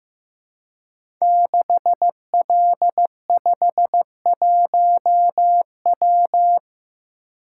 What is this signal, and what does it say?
Morse code, heard as a keyed tone, about 15 wpm, 700 Hz. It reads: 6L51W
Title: Code Groups - 5 Letter:Numbers - 15wpm